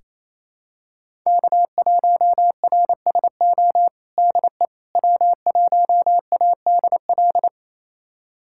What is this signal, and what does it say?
Morse code, heard as a keyed tone, about 28 wpm, 700 Hz. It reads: K1RSO DE W1ADL
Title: USA Callsign DE Callsign - 5 Characters - 28wpm